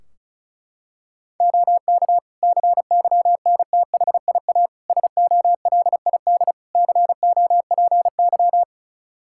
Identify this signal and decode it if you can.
Morse code, heard as a keyed tone, about 36 wpm, 700 Hz. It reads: OK CYNTHIA SOLID COPY
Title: K1USN SST QSO Practice 35wpm